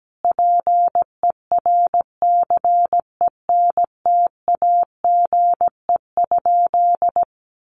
Morse code, heard as a keyed tone, about 17 wpm, 700 Hz. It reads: PERCENTAGE?
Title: Sentences from Top 1300 Words 17wpm